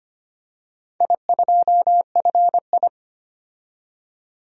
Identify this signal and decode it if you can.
Morse code, heard as a keyed tone, about 25 wpm, 700 Hz. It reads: I2FI
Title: DXCC-100 Call Signs - 4 Characters - 25wpm